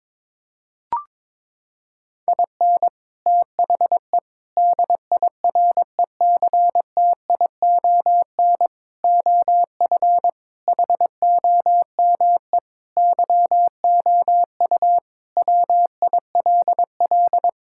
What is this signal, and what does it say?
Morse code, heard as a keyed tone, about 22 wpm, 700 Hz. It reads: IN THE DIRECTION OF HOME YOU WILL
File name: practice / 22wpm / Sentences with 4-Word Phrases - 22wpm